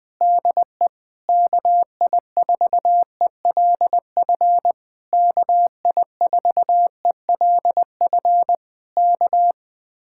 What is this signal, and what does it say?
Morse code, heard as a keyed tone, about 20 wpm, 700 Hz. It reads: DE KI4ELF KI4ELF K
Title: Speed-Racing Common QSO Phrases Encore - 20wpm